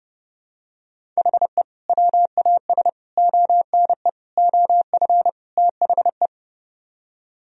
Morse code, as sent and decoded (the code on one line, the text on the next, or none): .... / .-- .- ... / --- -. / --- ..-. / - .... .
HE WAS ONE OF THE